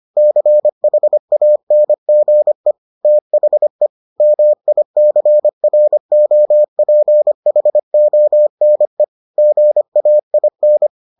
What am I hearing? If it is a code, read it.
Morse code, heard as a keyed tone, about 25 wpm, 600 Hz. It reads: CHANGE THE MICROPHONE GAIN